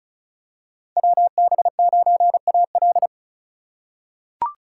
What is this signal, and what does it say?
Morse code, heard as a keyed tone, about 36 wpm, 700 Hz. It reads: WB9AL
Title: USA Call Signs - 3x Character Spacing - 35wpm